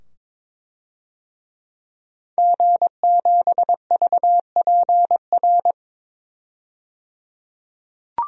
Morse code, heard as a keyed tone, about 22 wpm, 700 Hz. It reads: G7VPR
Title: DXCC-100 Call Signs - 5 Characters - 22wpm